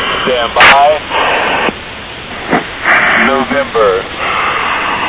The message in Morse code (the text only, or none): none